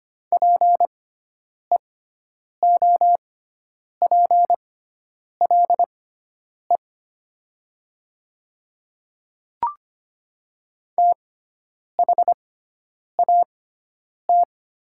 Morse code, heard as a keyed tone, about 25 wpm, spaced as for 9 wpm, 700 Hz. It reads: PEOPLE THAT
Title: 6x Character Spacing - Top 200 Words - 25wpm